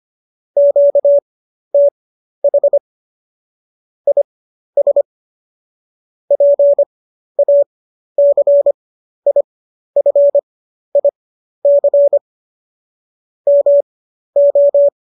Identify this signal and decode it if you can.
Morse code, heard as a keyed tone, about 25 wpm, spaced as for 12 wpm, 570 Hz. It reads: QTH IS PACIFIC MO
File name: qso008 12